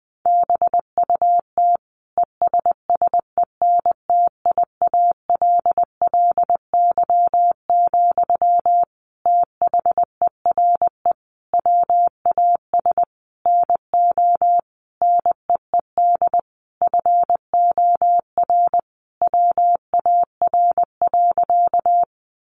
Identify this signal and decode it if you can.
Morse code, heard as a keyed tone, about 20 wpm, 700 Hz. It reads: BUT ESSENTIALLY, THERE WAS NO NEED FOR WAR.
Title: Sentences from Top 1300 Words 20wpm